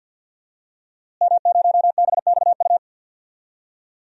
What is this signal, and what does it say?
Morse code, heard as a keyed tone, about 52 wpm, 700 Hz. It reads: M0BXU
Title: DXCC-100 Call Signs - 5 Characters - Repeated 3x - 50wpm